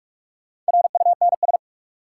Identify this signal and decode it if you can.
Morse code, heard as a keyed tone, about 42 wpm, 700 Hz. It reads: RUNS
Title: Sentences from Top 600 Words - New Word Drill - 45wpm